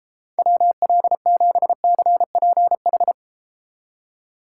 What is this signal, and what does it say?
Morse code, heard as a keyed tone, about 30 wpm, 700 Hz. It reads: WL7CPH